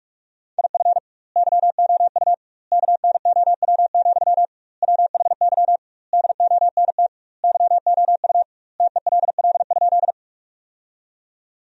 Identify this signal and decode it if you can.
Morse code, heard as a keyed tone, about 48 wpm, 700 Hz. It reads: IF YOU KNOW, WHY DONT YOU TELL?
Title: 2x Word-Spacing - Sentences from Top 300 Words - 45wpm